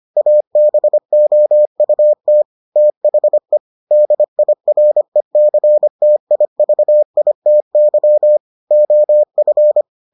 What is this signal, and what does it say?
Morse code, heard as a keyed tone, about 25 wpm, 600 Hz. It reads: ABOUT THE DIRECTIVITY OF